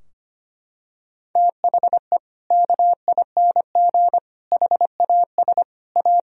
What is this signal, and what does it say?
Morse code, heard as a keyed tone, about 25 wpm, 700 Hz. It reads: THE KING HAS A
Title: Sentences from Top 1000 Words - Review Part 4 - 25wpm